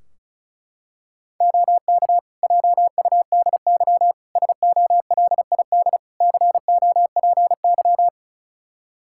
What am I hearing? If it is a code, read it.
Morse code, heard as a keyed tone, about 36 wpm, 700 Hz. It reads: OK JUDY SOLID COPY
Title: Practice QSO - Exchange 3 - More Info - 35wpm